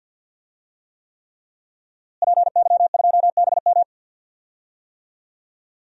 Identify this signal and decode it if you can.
Morse code, heard as a keyed tone, about 51 wpm, 700 Hz. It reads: PY2BM
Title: DXCC-100 Call Signs - 5 Characters - Repeated 3x - 50wpm